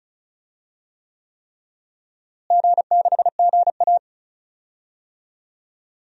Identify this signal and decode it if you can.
Morse code, heard as a keyed tone, about 34 wpm, 700 Hz. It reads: G6GA
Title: DXCC-100 Call Signs - 4 Characters - Repeated 3x - 35wpm